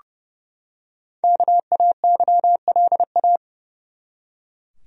Morse code, heard as a keyed tone, about 32 wpm, 700 Hz. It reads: KAYLA
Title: USA Common Names - Max 5 Letters - 30wpm